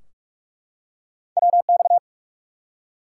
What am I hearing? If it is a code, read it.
Morse code, heard as a keyed tone, about 43 wpm, 700 Hz. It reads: WX
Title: Top 90 QSO Elements 45wpm